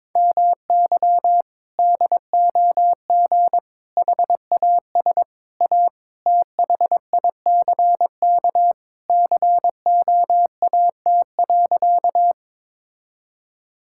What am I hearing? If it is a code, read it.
Morse code, heard as a keyed tone, about 22 wpm, 700 Hz. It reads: MY DOG HAS A THICK COAT.
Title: Sentences from Top 1600 Words 22wpm